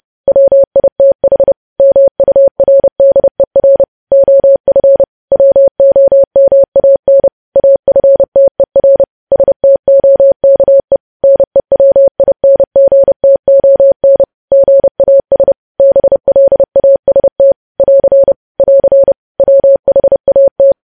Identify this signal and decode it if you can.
Morse code, heard as a keyed tone, about 30 wpm, 550 Hz. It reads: WITH MURDER OF WOMAN AFTER STOKE NEWINGTON GAS BLAST <AR> <AR> WHAT
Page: news headlines in morse code at 30 wpm